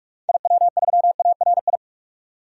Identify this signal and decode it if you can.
Morse code, heard as a keyed tone, about 46 wpm, 700 Hz. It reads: IW3ARI
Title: DXCC-100 Call Signs - 6 Characters - 45wpm